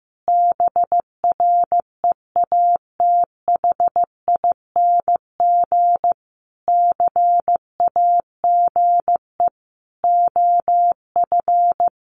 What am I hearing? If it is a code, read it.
Morse code, heard as a keyed tone, about 14 wpm, 700 Hz. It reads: BREATHING CAGE OF
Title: A Sound Of Thunder 15wpm